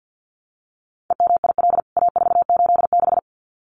Morse code, heard as a keyed tone, about 50 wpm, 700 Hz. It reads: EMIL A486
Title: CWT Exchange 50wpm